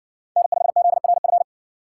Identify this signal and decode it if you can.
Morse code, heard as a keyed tone, about 60 wpm, 700 Hz. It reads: N5ZRF